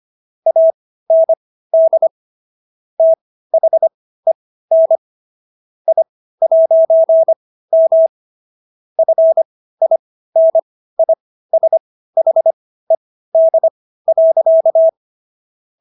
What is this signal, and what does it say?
Morse code, heard as a keyed tone, about 25 wpm, spaced as for 15 wpm, 650 Hz. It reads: AND THEN I'M FINISHED.